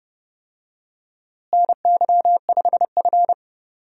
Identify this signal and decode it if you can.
Morse code, heard as a keyed tone, about 30 wpm, 700 Hz. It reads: NY5F
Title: USA Call Signs - 4 Characters - Repeated 3x - 30wpm